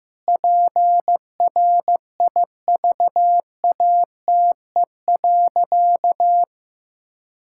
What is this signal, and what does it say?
Morse code, heard as a keyed tone, about 15 wpm, 700 Hz. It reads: PRIVATE.
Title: Sentences from Top 1600 Words 15wpm